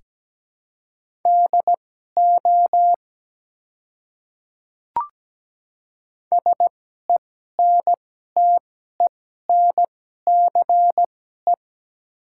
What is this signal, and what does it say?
Morse code, heard as a keyed tone, about 17 wpm, spaced as for 12 wpm, 700 Hz. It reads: DO SENTENCE
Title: 2x Character Spacing - Top 200 Words - 17wpm